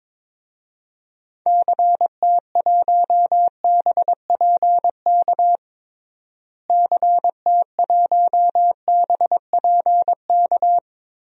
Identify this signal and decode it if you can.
Morse code, heard as a keyed tone, about 22 wpm, 700 Hz. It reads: CT1BPK CT1BPK
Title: DXCC-100 Call Signs - 6 Characters - Repeated 3x - 22wpm